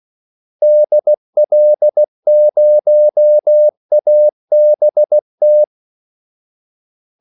Morse code, heard as a keyed tone, about 16 wpm, 600 Hz. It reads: DL0ABT
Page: Kaldesignaler 16 wpm